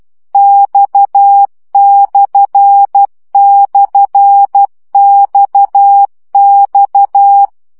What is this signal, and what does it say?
Morse code, heard as a keyed tone, about 12 wpm, 800 Hz. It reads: X//XX
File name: CW gyakorlas x per 10 baud